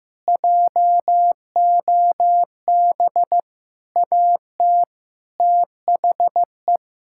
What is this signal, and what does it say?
Morse code, heard as a keyed tone, about 15 wpm, 700 Hz. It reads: JOB AT THE
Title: Sentences from Top 1600 Words 15wpm